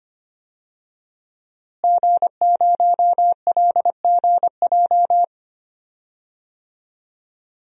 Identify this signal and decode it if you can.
Morse code, heard as a keyed tone, about 25 wpm, 700 Hz. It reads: G0LGJ
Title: DXCC-100 Call Signs - 5 Characters - Repeated 2x - 25wpm